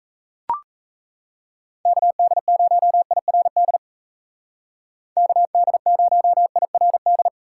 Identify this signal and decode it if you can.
Morse code, heard as a keyed tone, about 36 wpm, 700 Hz. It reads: KD0IRD KD0IRD